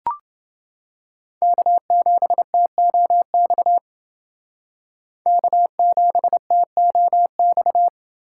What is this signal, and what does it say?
Morse code, heard as a keyed tone, about 30 wpm, 700 Hz. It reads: K7TOX K7TOX